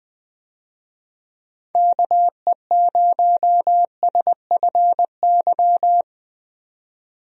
Morse code, heard as a keyed tone, about 20 wpm, 700 Hz. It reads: KE0SFY
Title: USA Call Signs - 6 Characters - 20wpm